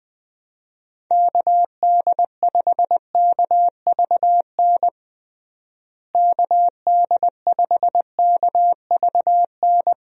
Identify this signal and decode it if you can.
Morse code, heard as a keyed tone, about 20 wpm, 700 Hz. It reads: KD5KVN KD5KVN
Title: USA Call Signs - 6 Characters - Repeated 3x - 20wpm